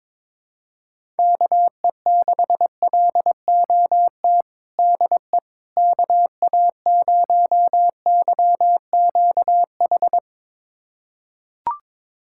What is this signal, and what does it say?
Morse code, heard as a keyed tone, about 22 wpm, 700 Hz. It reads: KE6LOT DE KA0YQH
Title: USA Callsign DE Callsign 22wpm